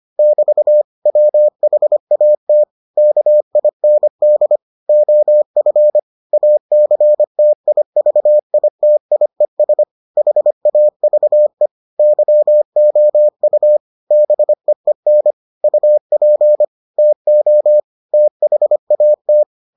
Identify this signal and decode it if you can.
Morse code, heard as a keyed tone, about 25 wpm, 600 Hz. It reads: <BT> WHAT KIND OF ACTIVITIES HAVE YOU BEEN UP TO THAT